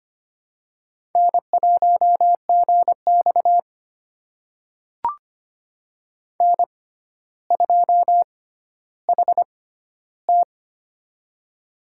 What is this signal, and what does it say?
Morse code, heard as a keyed tone, about 25 wpm, spaced as for 13 wpm, 700 Hz. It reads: N1GX N2HT
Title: USA Call Signs - 4 Characters - 6x Character Spacing - 25wpm